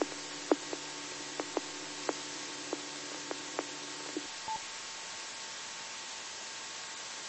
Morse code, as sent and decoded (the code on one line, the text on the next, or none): none